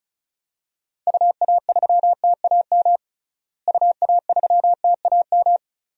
Morse code, heard as a keyed tone, about 36 wpm, 700 Hz. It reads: UA3TAM UA3TAM
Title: DXCC-100 Call Signs - 6 Characters - Repeated 2x - 35wpm